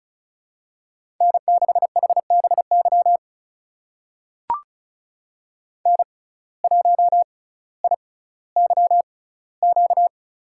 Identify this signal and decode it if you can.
Morse code, heard as a keyed tone, about 34 wpm, spaced as for 18 wpm, 700 Hz. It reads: N6HBY N1IYQ
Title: USA Call Signs - 6x Character Spacing - 35wpm